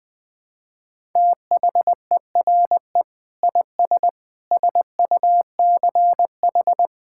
Morse code, heard as a keyed tone, about 21 wpm, 700 Hz. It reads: THERE IS SUCH
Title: Sets of 6 Words - 20wpm